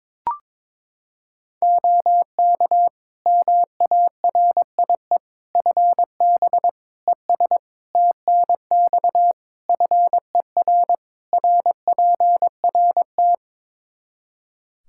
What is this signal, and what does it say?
Morse code, heard as a keyed tone, about 22 wpm, 700 Hz. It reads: OK MARIE FB ES TNX FER RPRT
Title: Practice QSO - Exchange 2 - Info - 22wpm